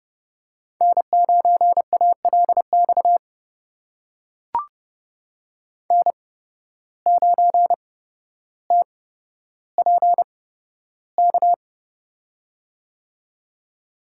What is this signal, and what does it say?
Morse code, heard as a keyed tone, about 30 wpm, spaced as for 13 wpm, 700 Hz. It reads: N9ALX N9TPK
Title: USA Call Signs - 5 Characters - 8x Character Spacing - 30wpm